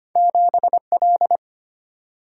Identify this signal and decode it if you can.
Morse code, heard as a keyed tone, about 25 wpm, 700 Hz. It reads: 7L
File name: Two Letter-Number Combinations - 25wpm